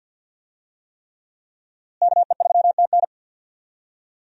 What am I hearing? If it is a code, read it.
Morse code, heard as a keyed tone, about 53 wpm, 700 Hz. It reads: KE3TN
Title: USA Call Signs - 5 Characters - Repeated 3x - 50wpm